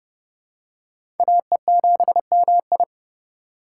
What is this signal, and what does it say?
Morse code, heard as a keyed tone, about 30 wpm, 700 Hz. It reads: AE7MI
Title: USA Call Signs - 5 Characters - 4x Character Spacing - 30wpm